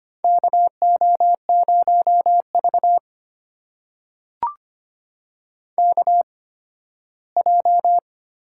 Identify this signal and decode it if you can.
Morse code, heard as a keyed tone, about 25 wpm, spaced as for 15 wpm, 700 Hz. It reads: KO0V KJ
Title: USA Call Signs - 8x Character Spacing - 25wpm